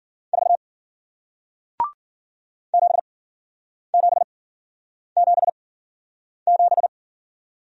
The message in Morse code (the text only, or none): 4 7777